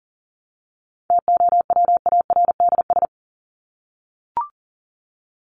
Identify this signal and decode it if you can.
Morse code, heard as a keyed tone, about 40 wpm, 700 Hz. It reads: TOWARDS
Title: ES:S Suffix - Top 100 Words with Base Words 40wpm